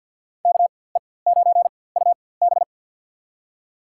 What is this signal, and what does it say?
Morse code, heard as a keyed tone, about 52 wpm, spaced as for 24 wpm, 700 Hz. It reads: KE9UB
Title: USA Call Signs - 5 Characters - 4x Character Spacing - 50wpm